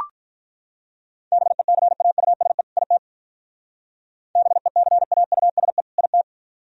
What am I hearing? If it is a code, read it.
Morse code, heard as a keyed tone, about 43 wpm, 700 Hz. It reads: BECAUSE IT BECAUSE IT
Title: Speed-Racing - Sets of 2 Words 35wpm